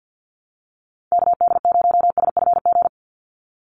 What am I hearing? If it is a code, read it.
Morse code, heard as a keyed tone, about 50 wpm, 700 Hz. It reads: KD0SFG
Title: USA Call Signs 50wpm